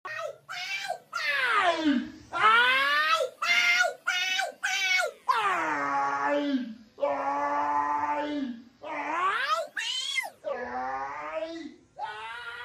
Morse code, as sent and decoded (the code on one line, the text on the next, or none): none